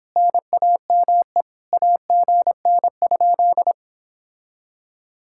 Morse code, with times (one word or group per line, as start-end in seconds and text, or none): 0.2-1.4 NAME
1.7-3.7 AGN?